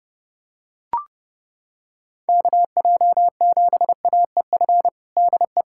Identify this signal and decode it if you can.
Morse code, heard as a keyed tone, about 29 wpm, 700 Hz. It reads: KJ7AEF DE
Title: USA Callsign DE Callsign 30wpm